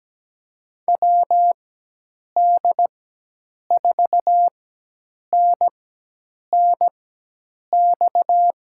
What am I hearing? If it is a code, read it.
Morse code, heard as a keyed tone, about 17 wpm, spaced as for 8 wpm, 700 Hz. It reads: WD4NNX
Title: USA Call Signs - 6 Characters - 4x Character Spacing - 17wpm